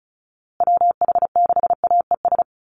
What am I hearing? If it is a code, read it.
Morse code, heard as a keyed tone, about 34 wpm, 700 Hz. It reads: WH6AES
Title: USA Call Signs Repeated 3x 35wpm